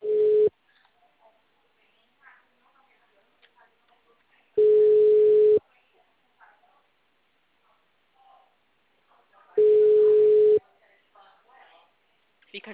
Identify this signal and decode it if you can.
Ringback tone (Europe, 425 Hz), with speech.
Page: Ribeirao sala 906